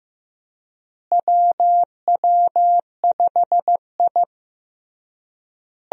Morse code, heard as a keyed tone, about 15 wpm, 700 Hz. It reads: WW5I
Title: USA Call Signs - 4 Characters - Repeated 3x - 15wpm